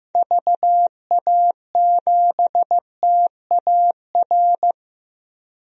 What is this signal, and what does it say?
Morse code, heard as a keyed tone, about 15 wpm, 700 Hz. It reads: VA7TAR
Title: DXCC-100 Call Signs - 6 Characters - Repeated 3x - 15wpm